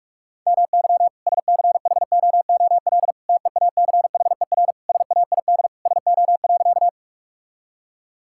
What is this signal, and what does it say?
Morse code, heard as a keyed tone, about 46 wpm, 690 Hz. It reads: MY SCHOOL TEACHER SAID SO.
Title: Sentences from Top 1000 Words - Review Part 6 - 45wpm